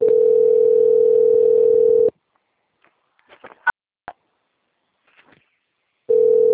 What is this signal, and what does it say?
Ringback tone (North America), 440 + 480 Hz.